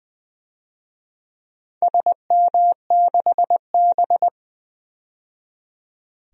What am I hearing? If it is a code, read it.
Morse code, heard as a keyed tone, about 20 wpm, 700 Hz. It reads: SM6B
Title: DXCC-100 Call Signs - 4 Characters - 20wpm